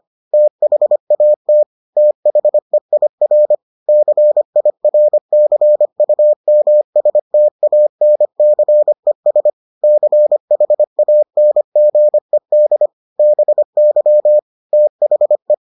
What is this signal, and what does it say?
Morse code, heard as a keyed tone, about 25 wpm, 600 Hz. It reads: THAT THEIR CIRCUMSTANCES CHANGED BY THE